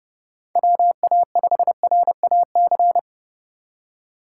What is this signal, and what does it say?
Morse code, heard as a keyed tone, about 30 wpm, 700 Hz. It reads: WA5RAC
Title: USA Call Signs - 2x Character Spacing - 30wpm